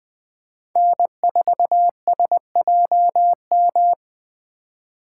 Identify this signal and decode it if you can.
Morse code, heard as a keyed tone, about 20 wpm, 700 Hz. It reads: N4SJM